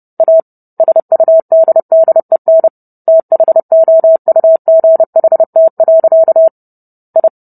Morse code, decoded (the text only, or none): A SUDDEN THOUGHT. I